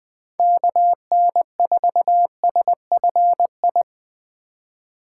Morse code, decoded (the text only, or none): KN4SFI